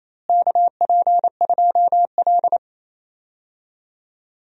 Morse code, heard as a keyed tone, about 28 wpm, 700 Hz. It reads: KP2L
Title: DXCC-100 Call Signs - 4 Characters - Repeated 2x - 28wpm